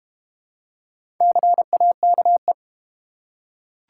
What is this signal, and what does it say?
Morse code, heard as a keyed tone, about 32 wpm, 690 Hz. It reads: CAKE